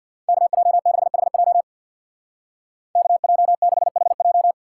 Morse code, heard as a keyed tone, about 55 wpm, 700 Hz. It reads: KJ6HJ KJ6HJ
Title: Speed Racing - USA 2x2 Call Signs 50wpm